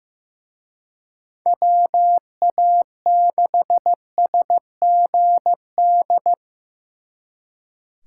Morse code, heard as a keyed tone, about 18 wpm, 700 Hz. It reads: WA6SGD
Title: Speed Racing - USA Call Signs 15wpm